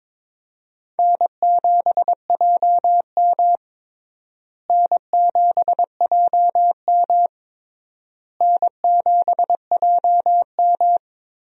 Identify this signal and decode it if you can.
Morse code, heard as a keyed tone, about 22 wpm, 700 Hz. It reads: N7JM N7JM N7JM
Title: USA Call Signs - 4 Characters - Repeated 3x - 22wpm